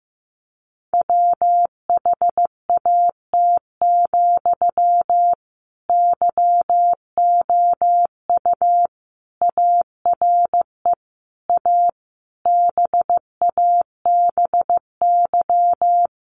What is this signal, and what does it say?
Morse code, heard as a keyed tone, about 15 wpm, 700 Hz. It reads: WHAT, YOU ARE A BABY
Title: Sentences from Top 1300 Words 15wpm